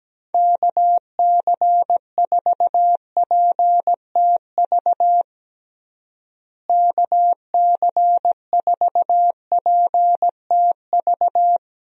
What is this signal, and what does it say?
Morse code, heard as a keyed tone, about 17 wpm, 700 Hz. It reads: KC4PTV KC4PTV
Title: USA Call Signs - 6 Characters - Repeated 2x - 17wpm